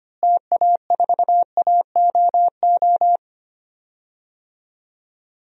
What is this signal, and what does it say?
Morse code, heard as a keyed tone, about 25 wpm, 700 Hz. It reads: TA4AOO
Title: DXCC-100 Call Signs - 6 Characters - Repeated 3x - 25wpm